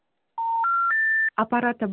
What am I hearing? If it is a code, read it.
Special information tone, with speech.